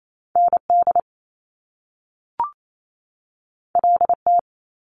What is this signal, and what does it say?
Morse code, heard as a keyed tone, about 28 wpm, 700 Hz. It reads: N D L T
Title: Top Two Letter Combinations in Words 28wpm